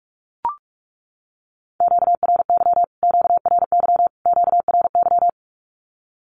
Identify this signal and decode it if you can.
Morse code, heard as a keyed tone, about 44 wpm, 700 Hz. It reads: QRY QRY QRY
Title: Q-Codes Repeated 3x 45wpm